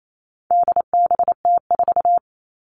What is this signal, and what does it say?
Morse code, heard as a keyed tone, about 28 wpm, 700 Hz. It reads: DBT4
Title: All Cut Numbers - 4-Digit 28wpm